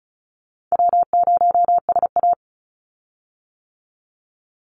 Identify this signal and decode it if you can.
Morse code, heard as a keyed tone, about 34 wpm, 700 Hz. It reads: W0SA
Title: Call Signs - 35wpm